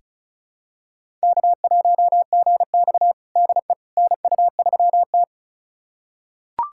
Morse code, heard as a keyed tone, about 36 wpm, 700 Hz. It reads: K1GX DE NU3T
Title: USA Callsign DE Callsign - 4 Characters - 35wpm